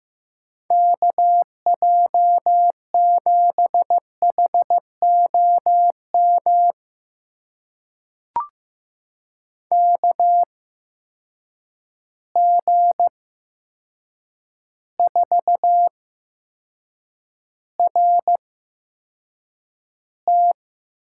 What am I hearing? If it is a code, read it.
Morse code, heard as a keyed tone, about 15 wpm, spaced as for 7 wpm, 700 Hz. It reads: KJ7HOM KG4RT
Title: USA Call Signs - 6 Characters - 8x Character Spacing - 15wpm